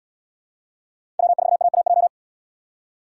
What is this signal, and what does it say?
Morse code, heard as a keyed tone, about 71 wpm, 690 Hz. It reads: W4AAJ